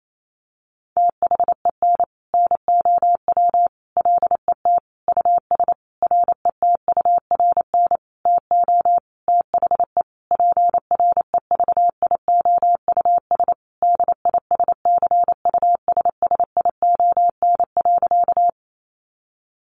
Morse code, heard as a keyed tone, about 28 wpm, 700 Hz. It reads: THEN NOW LET US RETURN TO THE PREVIOUS DISCUSSION.
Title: Sentences from Top 1500 Words 28wpm